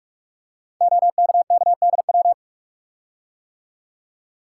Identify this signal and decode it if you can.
Morse code, heard as a keyed tone, about 44 wpm, 700 Hz. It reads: OKKDW
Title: Code Groups - 5 Letter:Numbers - 45wpm